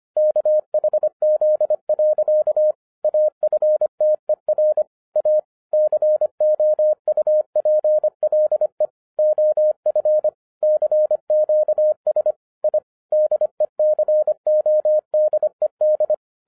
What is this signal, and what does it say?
Morse code, heard as a keyed tone, about 25 wpm, 610 Hz. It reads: KHZ. AFTER A COUPLE OF CQS I DECODED
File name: 2012-01-mp3 / n06-FIRST FISTS CONTACT-25wpm